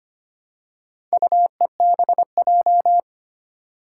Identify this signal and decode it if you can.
Morse code, heard as a keyed tone, about 25 wpm, 700 Hz. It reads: UEBJ
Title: Four Challenging Letter-Number Combinations 25wpm